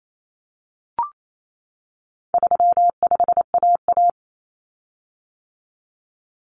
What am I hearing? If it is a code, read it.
Morse code, heard as a keyed tone, about 28 wpm, 700 Hz. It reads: 35AA
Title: Common Cut Numbers - 4-Digit 28wpm